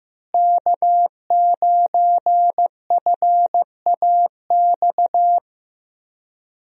Morse code, heard as a keyed tone, about 15 wpm, 700 Hz. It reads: K9FAX
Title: USA Call Signs - 5 Characters - Repeated 3x - 15wpm